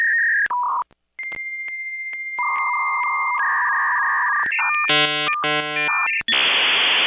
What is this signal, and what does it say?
Dial-up modem handshake.